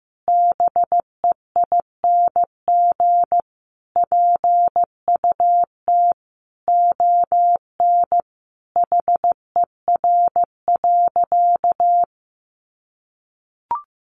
Morse code, heard as a keyed tone, about 14 wpm, 700 Hz. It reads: BEING PUT ON HER.
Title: Sentences from Top 1300 Words 15wpm